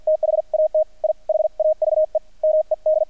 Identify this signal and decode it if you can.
Morse code, heard as a keyed tone, about 48 wpm, spaced as for 42 wpm, 620 Hz, with radio noise. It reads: THAT I HAVE MED
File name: archived / CRSnet